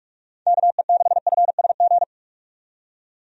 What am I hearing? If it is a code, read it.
Morse code, heard as a keyed tone, about 46 wpm, 700 Hz. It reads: KE6FSG
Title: USA Call Signs - 6 Characters - 45wpm